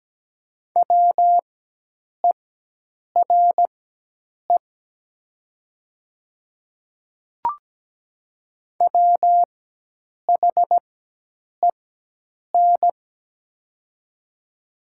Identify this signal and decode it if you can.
Morse code, heard as a keyed tone, about 17 wpm, spaced as for 8 wpm, 700 Hz. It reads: WERE WHEN
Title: 4x Character Spacing - Top 500 Words - 17wpm